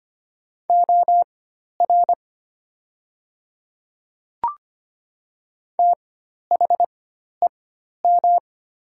Morse code, heard as a keyed tone, about 25 wpm, spaced as for 12 wpm, 700 Hz. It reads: OR THEM
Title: 4x Character Spacing - Top 100 Words - 25wpm